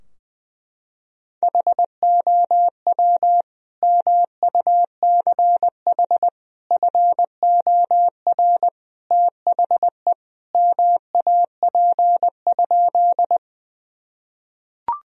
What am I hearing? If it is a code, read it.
Morse code, heard as a keyed tone, about 20 wpm, 700 Hz. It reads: HOW MUCH FOR THE MAP?
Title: Sentences from Top 1000 Words - Review Part 2 - 20wpm